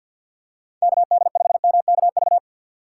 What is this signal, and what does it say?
Morse code, heard as a keyed tone, about 53 wpm, 700 Hz. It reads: KD5MKV
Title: USA Call Signs - 6x Character Spacing - 50wpm